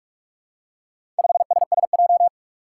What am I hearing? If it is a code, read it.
Morse code, heard as a keyed tone, about 44 wpm, 700 Hz. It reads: 5SSJ
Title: Four Challenging Letter-Number Combinations 45wpm